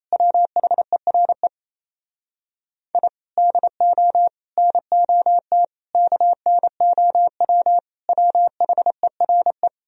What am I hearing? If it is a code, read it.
Morse code, heard as a keyed tone, about 37 wpm, 700 Hz. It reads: WHERE I DO NOT KNOW WHERE